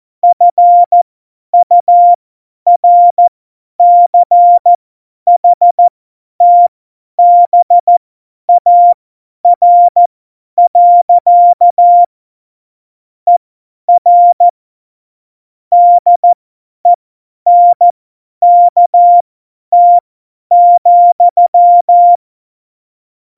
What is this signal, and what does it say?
Morse code, heard as a keyed tone, about 14 wpm, spaced as for 10 wpm, 700 Hz. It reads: FURCHTBAR. ER DENKT,
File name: Morsekurs / Witze / Witz 27 Juni 17, Tempo 50 F70